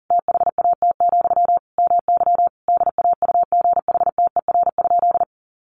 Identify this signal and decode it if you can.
Morse code, heard as a keyed tone, about 40 wpm, 700 Hz. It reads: THAT, MY DAUGHTER?
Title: Sentences from Top 1500 Words 40wpm